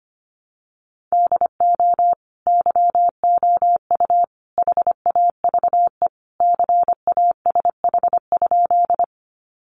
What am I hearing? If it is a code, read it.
Morse code, heard as a keyed tone, about 25 wpm, 700 Hz. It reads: DO YOU HAVE CASH?
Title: Sentences from Top 700 Words 25wpm